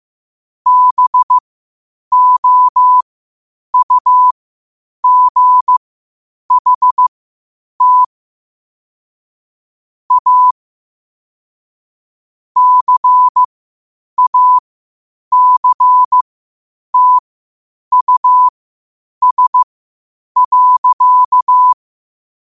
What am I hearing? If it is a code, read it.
Morse code, heard as a keyed tone, about 15 wpm, spaced as for 8 wpm, 1000 Hz. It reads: BOUGHT A CACTUS.
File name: qotdfiles / 8wpm